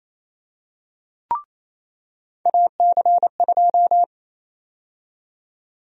Morse code, heard as a keyed tone, about 27 wpm, 700 Hz. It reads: AC2
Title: USA Call Sign Prefixes - 28wpm